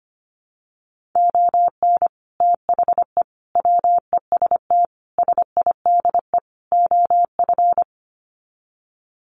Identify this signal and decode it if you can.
Morse code, heard as a keyed tone, about 25 wpm, 700 Hz. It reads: ON THE WEST SIDE OF
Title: Sets of 5 Words Encore2 25wpm